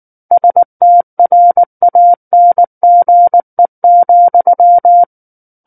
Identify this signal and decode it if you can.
Morse code, heard as a keyed tone, about 19 wpm, 700 Hz. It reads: STRANGE,
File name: Book One / 10-Chapter 10 - 19 WPM